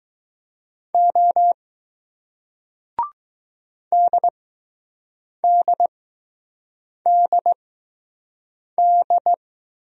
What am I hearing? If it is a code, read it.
Morse code, heard as a keyed tone, about 20 wpm, spaced as for 6 wpm, 700 Hz. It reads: O DDDD